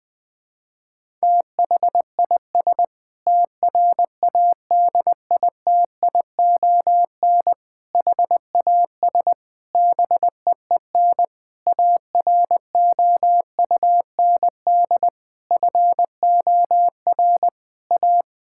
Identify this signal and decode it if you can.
Morse code, heard as a keyed tone, about 20 wpm, 700 Hz. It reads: THIS TRADITION HAS BEEN AROUND FOR A
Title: Sentences with 7-Word Phrases - Spoken Beginning - 20wpm